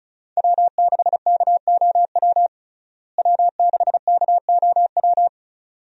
Morse code, heard as a keyed tone, about 36 wpm, 700 Hz. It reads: W6KOW W6KOW
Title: USA Call Signs - 5 Characters - Repeated 2x - 35wpm